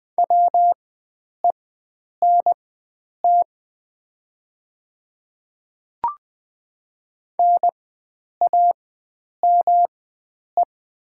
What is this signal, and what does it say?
Morse code, heard as a keyed tone, about 20 wpm, spaced as for 9 wpm, 700 Hz. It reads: WENT NAME
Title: 4x Character Spacing - Top 300 Words - 20wpm